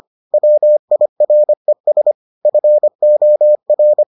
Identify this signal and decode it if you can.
Morse code, heard as a keyed tone, about 25 wpm, 600 Hz. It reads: WIRES FOR